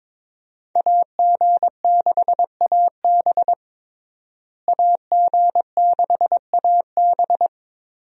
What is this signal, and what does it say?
Morse code, heard as a keyed tone, about 22 wpm, 700 Hz. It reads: AG6AB AG6AB
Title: USA Call Signs - 5 Characters - Repeated 3x - 22wpm